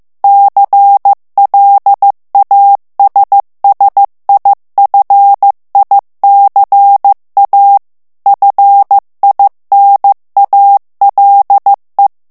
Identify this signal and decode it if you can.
Morse code, heard as a keyed tone, about 15 wpm, 800 Hz. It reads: CLASSIFICA FINALE
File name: morse mp3 / ITALIEN texte clair 2 vitesse 14 mots minute